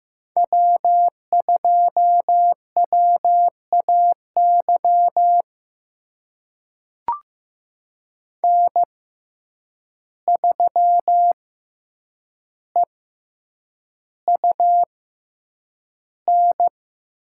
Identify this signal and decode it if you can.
Morse code, heard as a keyed tone, about 15 wpm, spaced as for 8 wpm, 700 Hz. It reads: W2WAY N3EUN
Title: USA Call Signs - 5 Characters - 6x Character Spacing - 15wpm